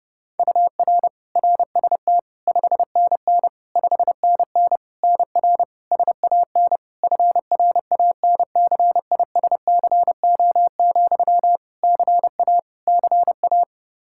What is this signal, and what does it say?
Morse code, heard as a keyed tone, about 32 wpm, 700 Hz. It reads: UR RST 5NN 5NN NR SAN FRANCISCO, CA CA
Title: Practice QSO - Exchange 1 - Beginning - 30wpm